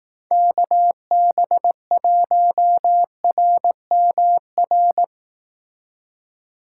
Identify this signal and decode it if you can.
Morse code, heard as a keyed tone, about 18 wpm, 700 Hz. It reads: KB1RMR
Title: Speed Racing - USA 2x3 Call Signs 15wpm